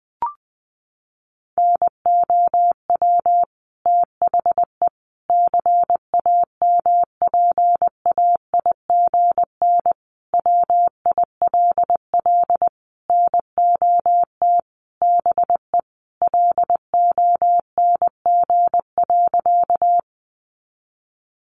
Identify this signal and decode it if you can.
Morse code, heard as a keyed tone, about 20 wpm, 700 Hz. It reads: NOW THE CAMPAIGN WILL NOT BE LONG.
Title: Sentences from Top 1200 Words 20wpm